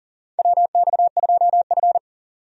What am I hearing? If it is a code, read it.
Morse code, heard as a keyed tone, about 40 wpm, 700 Hz. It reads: WX2F
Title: USA Call Signs - 4 Characters - 40wpm